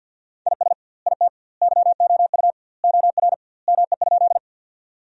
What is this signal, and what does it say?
Morse code, heard as a keyed tone, about 50 wpm, 700 Hz. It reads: IS IT YOU OR ME?
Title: 2x Word-Spacing - Sentences from Top 200 Words - 50wpm